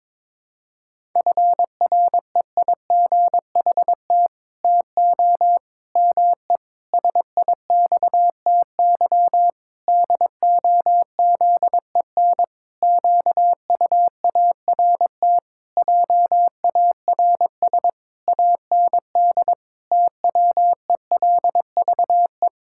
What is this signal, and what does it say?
Morse code, heard as a keyed tone, about 22 wpm, 700 Hz. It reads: FREIGHT TO ME SIXTY DOZEN QUART JARS AND TWELVE
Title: Pangram Challenge v2 - 22wpm